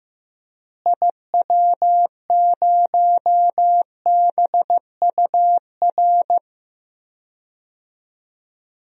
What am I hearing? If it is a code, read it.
Morse code, heard as a keyed tone, about 14 wpm, 700 Hz. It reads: IW0BUR
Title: DXCC-100 Call Signs - 6 Characters - Repeated 3x - 15wpm